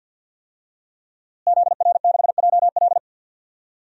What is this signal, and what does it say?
Morse code, heard as a keyed tone, about 52 wpm, 700 Hz. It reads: ZR6JL
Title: DXCC-100 Call Signs - 5 Characters - Repeated 2x - 50wpm